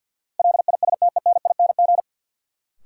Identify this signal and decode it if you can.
Morse code, heard as a keyed tone, about 49 wpm, 700 Hz. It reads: LISTENING
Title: Sets of 5 Words - New Word Drill - 50wpm